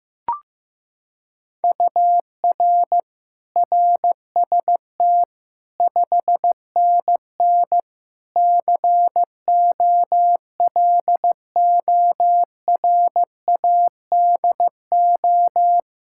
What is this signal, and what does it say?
Morse code, heard as a keyed tone, about 15 wpm, 700 Hz. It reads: UR RST 5NN COLORADO
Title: Practice QSO - Exchange 1 - Beginning - 15wpm